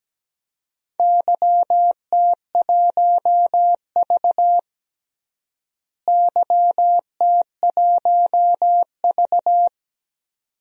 Morse code, heard as a keyed tone, about 17 wpm, 700 Hz. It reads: YT1V YT1V
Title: DXCC-100 Call Signs - 4 Characters - Repeated 2x - 17wpm